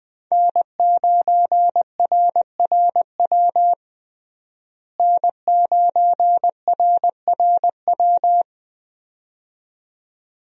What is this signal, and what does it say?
Morse code, heard as a keyed tone, about 20 wpm, 700 Hz. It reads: N9RRW N9RRW
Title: USA Call Signs - 5 Characters - Repeated 2x - 20wpm